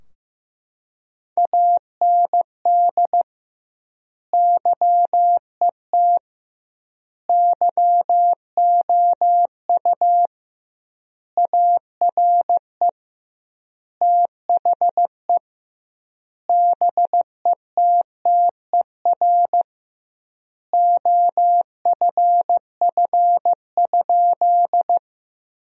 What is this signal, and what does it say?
Morse code, heard as a keyed tone, about 15 wpm, 700 Hz. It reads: AND YET YOU ARE THE BETTER OFF?
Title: 2x Word-Spacing - Sentences from Top 500 Words - 15wpm